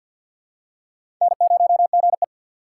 Morse code, heard as a keyed tone, about 47 wpm, 700 Hz. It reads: N0GE
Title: USA Call Signs - 4 Characters - 6x Character Spacing - 50wpm